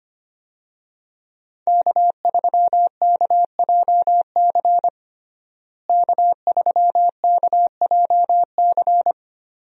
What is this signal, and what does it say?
Morse code, heard as a keyed tone, about 25 wpm, 700 Hz. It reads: K3KJC K3KJC
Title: USA Call Signs - 5 Characters - Repeated 2x - 25wpm